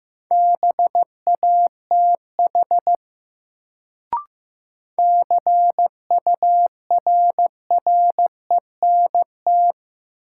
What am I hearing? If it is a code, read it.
Morse code, heard as a keyed tone, about 15 wpm, 700 Hz. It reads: BATH CURRENT
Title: Sentences from Top 1600 Words - New Word Drill Alt - 15wpm